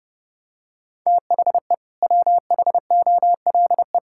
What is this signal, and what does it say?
Morse code, heard as a keyed tone, about 32 wpm, 700 Hz. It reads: THE WHOLE
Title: Sentences from Top 1000 Words - Review Part 5 - 30wpm